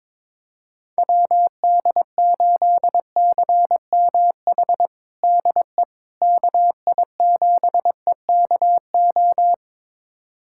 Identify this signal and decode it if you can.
Morse code, heard as a keyed tone, about 22 wpm, 700 Hz. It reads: WD8CMH DE KI7EKO
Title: USA Callsign DE Callsign 22wpm